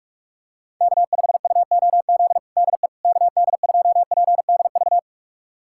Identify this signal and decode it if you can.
Morse code, heard as a keyed tone, about 46 wpm, 700 Hz. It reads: K5UOZ DE KD2PDV
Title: USA Callsign DE Callsign 45wpm